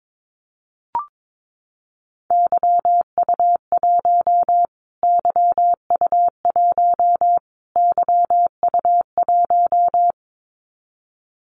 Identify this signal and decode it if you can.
Morse code, heard as a keyed tone, about 22 wpm, 700 Hz. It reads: YU1 YU1 YU1
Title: Callsign Prefixes for Top 25 DXCC - Repeated 3x - 22wpm